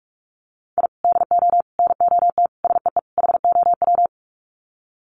Morse code, heard as a keyed tone, about 44 wpm, 700 Hz. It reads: I DO NOT SEE HOW
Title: Sets of 5 Words 45wpm